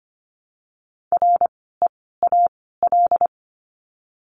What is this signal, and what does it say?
Morse code, heard as a keyed tone, about 25 wpm, 700 Hz. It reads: R E A L
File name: course / 25:10wpm / 094 Lesson 16f - Letter c - Words 25wpm